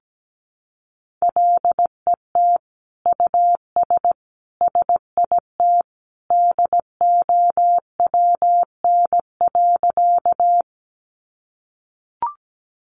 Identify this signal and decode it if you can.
Morse code, heard as a keyed tone, about 17 wpm, 700 Hz. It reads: LET US SIT DOWN.
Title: Sentences from Top 500 Words 17wpm